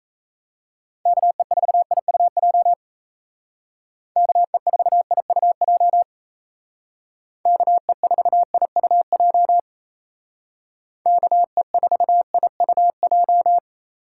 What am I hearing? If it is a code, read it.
Morse code, heard as a keyed tone, about 34 wpm, 700 Hz. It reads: KE4IUJ KE4IUJ KE4IUJ KE4IUJ
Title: Speed Racing - USA 2x3 Call Signs 28wpm